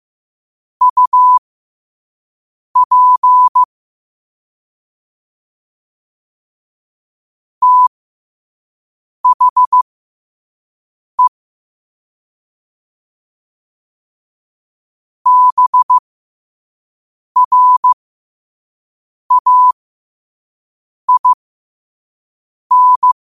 Morse code, heard as a keyed tone, about 15 wpm, spaced as for 5 wpm, 1000 Hz. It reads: UP THE BRAIN
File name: 2024-10-07-5wpm